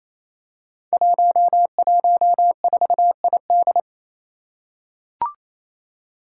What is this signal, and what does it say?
Morse code, heard as a keyed tone, about 28 wpm, 700 Hz. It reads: 114ID
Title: Five Challenging Letter-Number Combinations 28wpm